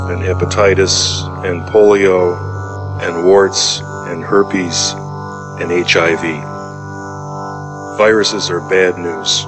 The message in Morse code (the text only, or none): none